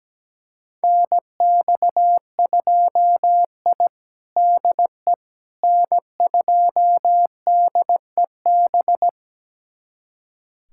Morse code, heard as a keyed tone, about 17 wpm, 700 Hz. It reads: NX2I DE N2DEB
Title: USA Callsign DE Callsign 17wpm